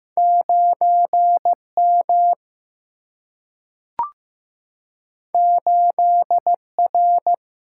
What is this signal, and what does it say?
Morse code, heard as a keyed tone, about 15 wpm, 700 Hz. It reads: 9 M 8 R
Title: Single-Letter Single-Number 15wpm